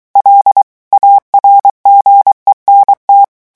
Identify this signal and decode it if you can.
Morse code, heard as a keyed tone, about 23 wpm, 800 Hz. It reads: L ARGENT
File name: morse mp3 / revision9 citations humoristiques vitesse 22 mots minute